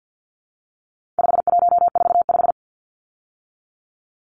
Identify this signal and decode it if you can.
Morse code, heard as a keyed tone, about 50 wpm, 700 Hz. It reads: 5145
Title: Numbers - 4-Digit 50wpm